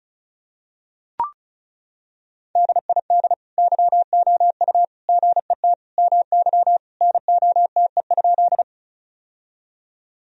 Morse code, heard as a keyed tone, about 36 wpm, 700 Hz. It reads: DID YOU GET MY NOTE?
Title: Sentences from Top 1000 Words - Review Part 2 - 35wpm